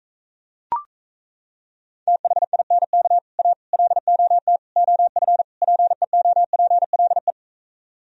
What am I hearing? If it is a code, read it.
Morse code, heard as a keyed tone, about 37 wpm, 700 Hz. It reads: THINK A LOT OF PEOPLE